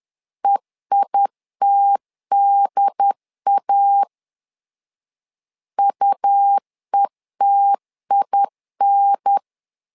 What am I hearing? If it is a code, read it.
Morse code, heard as a keyed tone, about 11 wpm, 800 Hz. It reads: EITDA UETIN
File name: LEZIONI / 3 MWRO / 3 M 10wpm 1828